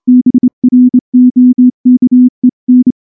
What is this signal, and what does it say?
Morse code, heard as a keyed tone, about 23 wpm, 250 Hz. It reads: BROKEN
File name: morse / morse-training-25WPM